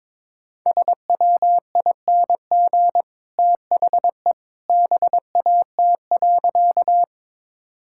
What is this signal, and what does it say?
Morse code, heard as a keyed tone, about 22 wpm, 700 Hz. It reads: SWING THE BAT.
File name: practice / 22wpm / Easy Sentences with Hints - 22wpm